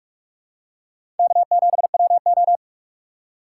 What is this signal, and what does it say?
Morse code, heard as a keyed tone, about 46 wpm, 700 Hz. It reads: K7WO
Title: USA Call Signs - 4 Characters - 2x Character Spacing - 45wpm